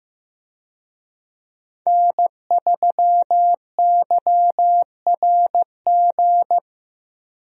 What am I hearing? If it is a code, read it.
Morse code, heard as a keyed tone, about 16 wpm, 700 Hz. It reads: N3YRG
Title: USA Call Signs - 5 Characters - Repeated 3x - 15wpm